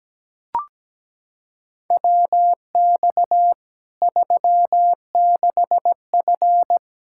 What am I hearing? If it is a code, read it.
Morse code, heard as a keyed tone, about 17 wpm, 700 Hz. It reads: WX 36F
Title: Practice QSO - Exchange 2 - Info - 17wpm